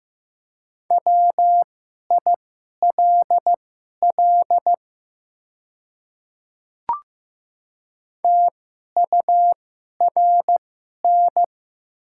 Morse code, heard as a keyed tone, about 15 wpm, spaced as for 11 wpm, 700 Hz. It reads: WILL TURN
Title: 2x Character Spacing - Top 200 Words - 15wpm